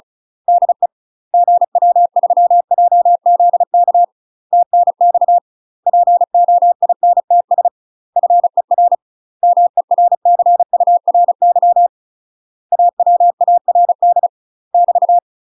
Morse code, heard as a keyed tone, about 35 wpm, 700 Hz. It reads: DE GW3JZK TNX POINTS FER MERCURY AWARD <BT>
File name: UK MASTER QSOs TRAINING / 35WPM / UK MASTER QSOs 0023